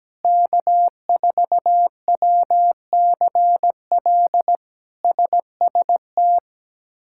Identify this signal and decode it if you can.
Morse code, heard as a keyed tone, about 17 wpm, 700 Hz. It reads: K4WCL SST
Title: K1USN SST QSO Practice - 17wpm